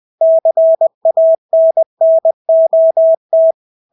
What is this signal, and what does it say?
Morse code, heard as a keyed tone, about 20 wpm, 650 Hz. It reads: CANNOT
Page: Engelske ord 20 wpm